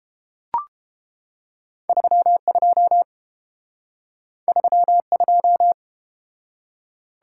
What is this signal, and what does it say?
Morse code, heard as a keyed tone, about 31 wpm, 700 Hz. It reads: 32 32